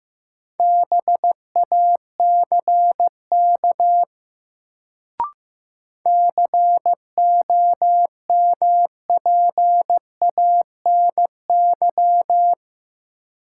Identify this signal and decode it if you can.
Morse code, heard as a keyed tone, about 15 wpm, 700 Hz. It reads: BACK COMPANY
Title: Sets of 2 Words - New Word Drill - 15wpm